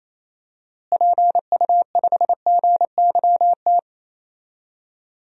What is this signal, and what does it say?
Morse code, heard as a keyed tone, about 28 wpm, 700 Hz. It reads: PU5GYT
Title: DXCC-100 Call Signs - 6 Characters - Repeated 2x - 28wpm